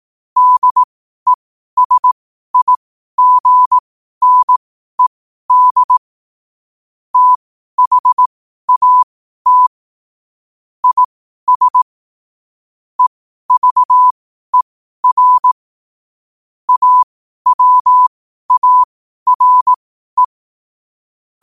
Morse code, heard as a keyed tone, about 18 wpm, spaced as for 12 wpm, 1000 Hz. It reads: DESIGNED THAT IS EVER AWARE
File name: qotdfiles / 12wpm